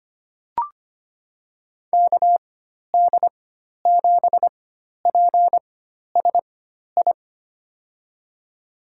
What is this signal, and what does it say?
Morse code, heard as a keyed tone, about 25 wpm, spaced as for 12 wpm, 700 Hz. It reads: KD7PSI
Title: USA Call Signs - 4x Character Spacing - 25wpm